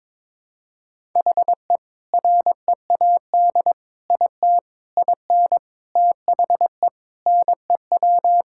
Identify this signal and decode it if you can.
Morse code, heard as a keyed tone, about 22 wpm, 700 Hz. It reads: HE READ IT IN THE NEW
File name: practice / 22wpm / Sentences with 4-Word Phrases - 22wpm